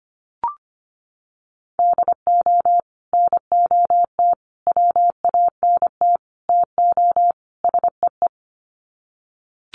Morse code, heard as a keyed tone, about 25 wpm, 700 Hz. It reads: DO NOT WANT TO SEE
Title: Sets of 5 Words 25wpm